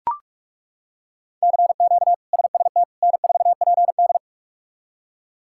Morse code, heard as a keyed tone, about 41 wpm, 700 Hz. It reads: CQ SST N4PD
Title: K1USN SST QSO Practice - 45wpm